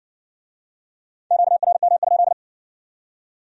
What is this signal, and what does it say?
Morse code, heard as a keyed tone, about 56 wpm, 700 Hz. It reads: QRM?